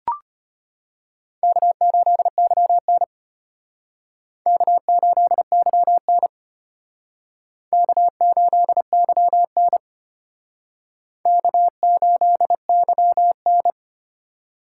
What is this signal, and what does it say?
Morse code, heard as a keyed tone, about 32 wpm, 700 Hz. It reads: K8YN K8YN K8YN K8YN